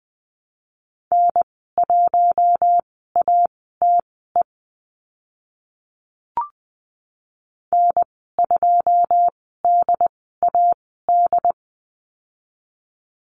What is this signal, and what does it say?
Morse code, heard as a keyed tone, about 20 wpm, spaced as for 14 wpm, 700 Hz. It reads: N1ATE N2DAD
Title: 082 Lesson 14g - Number 2 - Call Signs 20wpm